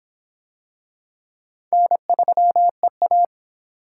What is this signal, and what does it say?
Morse code, heard as a keyed tone, about 26 wpm, 700 Hz. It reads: N3EA